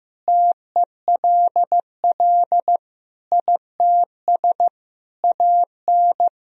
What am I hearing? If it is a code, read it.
Morse code, heard as a keyed tone, about 15 wpm, 700 Hz. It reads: TELL ITS AN
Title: Sentences from Top 1000 Words - Review Part 2 - 15wpm